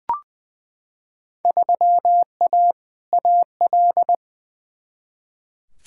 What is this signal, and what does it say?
Morse code, heard as a keyed tone, about 21 wpm, 700 Hz. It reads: 3A AL
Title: Field Day Exchanges - 20wpm